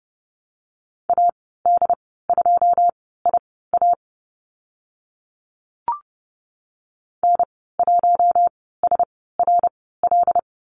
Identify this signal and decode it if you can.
Morse code, heard as a keyed tone, about 30 wpm, spaced as for 17 wpm, 700 Hz. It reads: AD2IA N1SRL
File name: Lesson 14g - Number 2 - Call Signs 30wpm